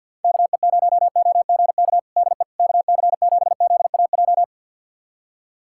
Morse code, heard as a keyed tone, about 53 wpm, 700 Hz. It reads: KE0OGK DE KC7ZAJ
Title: USA Callsign DE Callsign - 6 Characters - 50wpm